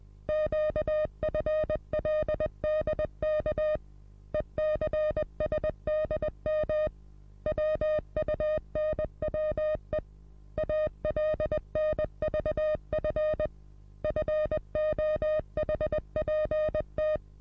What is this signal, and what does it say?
Morse code, heard as a keyed tone, about 20 wpm, 620 Hz. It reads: QFLDK ECSDM WUNWE ALNVF FOHPT
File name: repos / Morsekurs / DJ4UF Morsen27